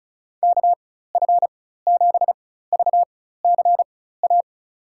Morse code, heard as a keyed tone, about 36 wpm, spaced as for 16 wpm, 700 Hz. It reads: KF7VCA
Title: USA Call Signs - 6 Characters - 4x Character Spacing - 35wpm